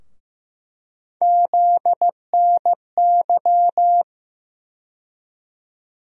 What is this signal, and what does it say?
Morse code, heard as a keyed tone, about 15 wpm, 700 Hz. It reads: ZNY
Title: USA Call Sign Postfixes - 3 Letters 15wpm